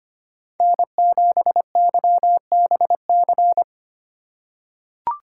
Morse code, heard as a keyed tone, about 25 wpm, 700 Hz. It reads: N7YBC
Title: USA Call Signs - 5 Characters - 2x Character Spacing - 25wpm